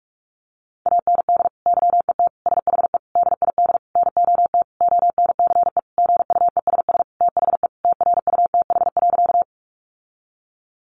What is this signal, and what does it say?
Morse code, heard as a keyed tone, about 44 wpm, 700 Hz. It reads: AND YET SHE DID NOT ONCE GUESS THE TRUTH.
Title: Sentences from Top 1500 Words 45wpm